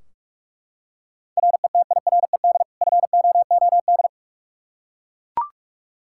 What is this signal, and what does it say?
Morse code, heard as a keyed tone, about 41 wpm, 700 Hz. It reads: RETIRED FOOD
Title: Practice QSO - Exchange 3 - More Info - 45wpm